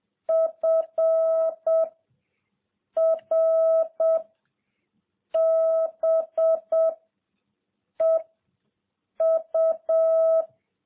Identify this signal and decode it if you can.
Morse code, heard as a keyed tone, about 7 wpm, 660 Hz. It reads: F R B E U